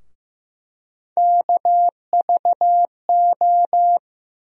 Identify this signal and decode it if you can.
Morse code, heard as a keyed tone, about 15 wpm, 700 Hz. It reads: KVO
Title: USA Call Sign Postfixes - 3 Letters 15wpm